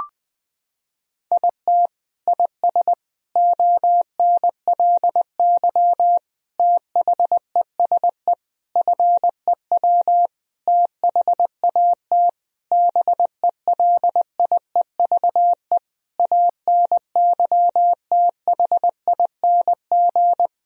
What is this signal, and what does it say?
Morse code, heard as a keyed tone, about 20 wpm, 700 Hz. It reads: IT IS ONLY THESE FEW THAT BELIEVE ANYTHING
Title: Sentences from Top 1000 Words - Review Part 1 - 20wpm